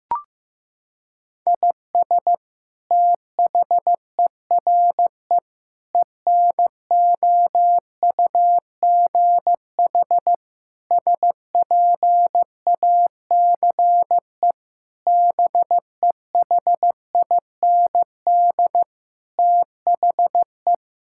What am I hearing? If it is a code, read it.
Morse code, heard as a keyed tone, about 15 wpm, 700 Hz. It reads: IS THERE ENOUGH SPACE BEHIND THE
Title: Sentences from Top 1000 Words - Review Part 5 - 15wpm